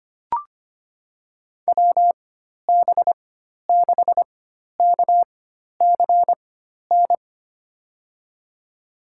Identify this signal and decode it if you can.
Morse code, heard as a keyed tone, about 25 wpm, spaced as for 12 wpm, 700 Hz. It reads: WB6KCN
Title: USA Call Signs - 6 Characters - 4x Character Spacing - 25wpm